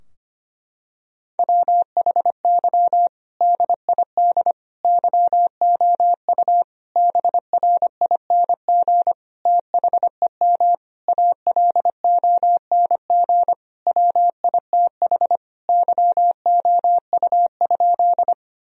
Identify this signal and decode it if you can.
Morse code, heard as a keyed tone, about 25 wpm, 700 Hz. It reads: WHY DID YOU BRING THEM ALONG WITH YOU?
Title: Sentences from Top 1000 Words - Review Part 3 - 25wpm